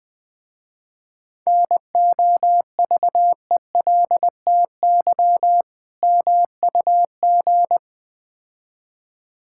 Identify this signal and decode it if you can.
Morse code, heard as a keyed tone, about 20 wpm, 700 Hz. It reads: NOVELTY MUG
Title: Difficult Words with Hints - 20wpm